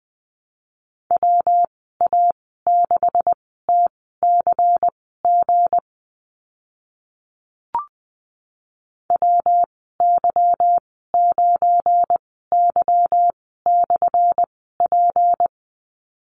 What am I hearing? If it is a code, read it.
Morse code, heard as a keyed tone, about 20 wpm, spaced as for 14 wpm, 700 Hz. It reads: WA6TCG WY9Y/P
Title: 179 Lesson 29g - Slash - Call Signs 20wpm